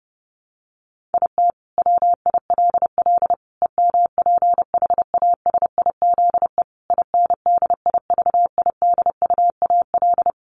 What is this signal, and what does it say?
Morse code, heard as a keyed tone, about 28 wpm, 700 Hz. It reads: IT WILL EMPHASIZE INDIVIDUAL
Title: Sentences from Top 1500 Words 30wpm